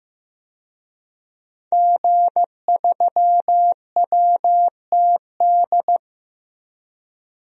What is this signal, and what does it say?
Morse code, heard as a keyed tone, about 15 wpm, 700 Hz. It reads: G3WTD
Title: DXCC-100 Call Signs - 5 Characters - 15wpm